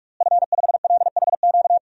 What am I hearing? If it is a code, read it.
Morse code, heard as a keyed tone, about 46 wpm, 700 Hz. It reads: F5LHQ
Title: DXCC-100 Call Signs - 5 Characters - Repeated 3x - 45wpm